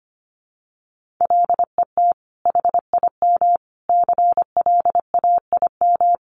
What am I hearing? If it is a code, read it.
Morse code, heard as a keyed tone, about 25 wpm, 700 Hz. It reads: LET HIM CLAIM
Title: Sentences from Top 1500 Words 25wpm